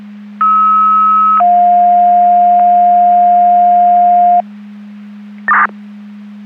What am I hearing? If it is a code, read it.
Two-tone page, heard as a one-second tone, then a three-second tone, about 1290 then 710 Hz. It is heard from 0.4 to 4.4 s.
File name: Portage / 00744 11-00 Portage 154.385 Paging